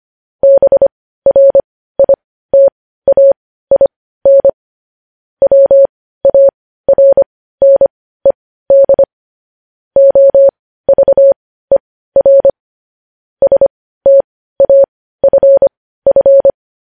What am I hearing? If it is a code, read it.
Morse code, heard as a keyed tone, about 25 wpm, spaced as for 15 wpm, 550 Hz. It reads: BRITAIN WARNED OVER STAFF
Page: news-headlines-in-morse-code-at-15-wpm